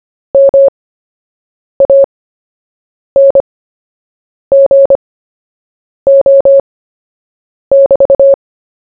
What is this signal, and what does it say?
Morse code, heard as a keyed tone, about 25 wpm, spaced as for 7 wpm, 550 Hz. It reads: MANGO<BT>
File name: cw / fists / Short-Story-1-7wpm